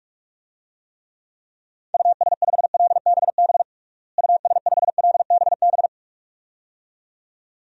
Morse code, heard as a keyed tone, about 46 wpm, 700 Hz. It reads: US5LBB US5LBB
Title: DXCC-100 Call Signs - 6 Characters - Repeated 2x - 45wpm